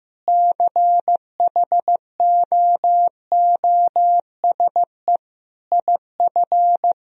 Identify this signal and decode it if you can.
Morse code, heard as a keyed tone, about 16 wpm, 700 Hz. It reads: CHOOSE IF
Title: Sentences from Top 1000 Words - Review Part 4 - 15wpm